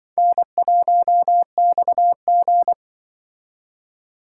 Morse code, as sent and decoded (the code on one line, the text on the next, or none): -. .---- -..- --.
N1XG